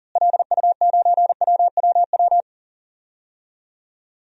Morse code, heard as a keyed tone, about 40 wpm, 700 Hz. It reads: LU9WWW
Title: DXCC-100 Call Signs - 6 Characters - Repeated 2x - 40wpm